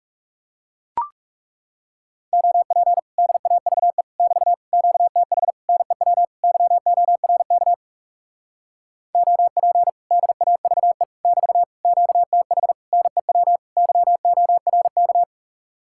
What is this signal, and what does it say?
Morse code, heard as a keyed tone, about 37 wpm, 700 Hz. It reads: OP DAVE <BT> QTH NEW YORK OP DAVE <BT> QTH NEW YORK
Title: Speed-Racing Common QSO Phrases Encore - 30wpm